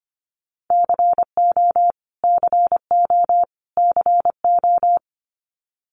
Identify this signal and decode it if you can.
Morse code, heard as a keyed tone, about 25 wpm, 700 Hz. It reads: CO CO CO
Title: US States Repeated 3x 25wpm